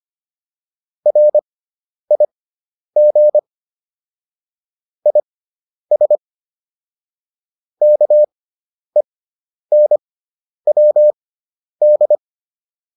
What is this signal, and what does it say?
Morse code, heard as a keyed tone, about 25 wpm, spaced as for 10 wpm, 600 Hz. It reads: RIG IS KENWD
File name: QSO309 10